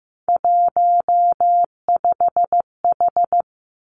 Morse code, heard as a keyed tone, about 15 wpm, 700 Hz. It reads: 15H
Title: Three Challenging Letter-Number Combinations 15wpm